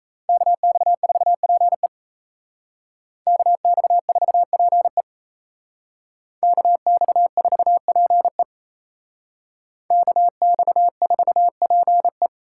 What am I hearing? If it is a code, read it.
Morse code, heard as a keyed tone, about 35 wpm, 700 Hz. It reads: KX4PE KX4PE KX4PE KX4PE